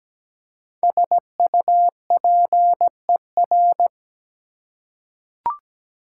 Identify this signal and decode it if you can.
Morse code, heard as a keyed tone, about 17 wpm, 700 Hz. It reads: SUPER
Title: Prefixes and Suffixes 17wpm